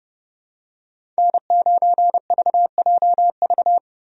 Morse code, heard as a keyed tone, about 30 wpm, 700 Hz. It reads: N9VJV
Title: USA Call Signs - 5 Characters - 30wpm